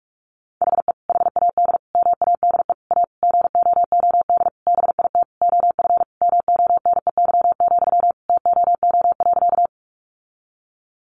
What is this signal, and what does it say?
Morse code, heard as a keyed tone, about 44 wpm, 700 Hz. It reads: HE HAD MADE A GOOD BIT OF MONEY, TOO.
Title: Sentences from Top 900 Words 45wpm